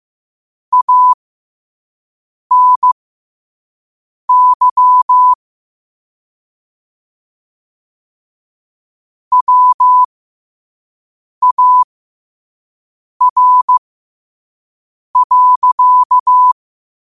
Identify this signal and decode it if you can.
Morse code, heard as a keyed tone, about 15 wpm, spaced as for 5 wpm, 1000 Hz. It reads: ANY WAR.
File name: qotdfiles / 5wpm